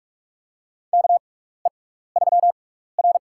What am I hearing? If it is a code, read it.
Morse code, heard as a keyed tone, about 42 wpm, spaced as for 15 wpm, 700 Hz. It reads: KE3R
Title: USA Call Signs - 4 Characters - 6x Character Spacing - 45wpm